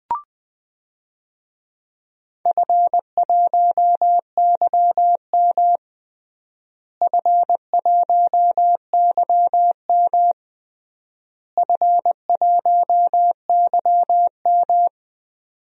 Morse code, heard as a keyed tone, about 19 wpm, 700 Hz. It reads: F1YM F1YM F1YM
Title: DXCC-100 Call Signs - 4 Characters - Repeated 3x - 20wpm